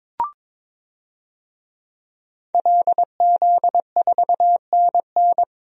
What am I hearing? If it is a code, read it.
Morse code, heard as a keyed tone, about 22 wpm, 700 Hz. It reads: LZ4NN
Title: DXCC-100 Call Signs - 5 Characters - 22wpm